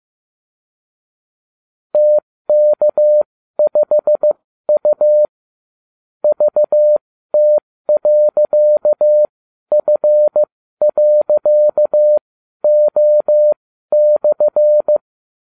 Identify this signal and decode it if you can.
Morse code, heard as a keyed tone, about 15 wpm, spaced as for 12 wpm, 600 Hz. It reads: TK5U VT.F.O/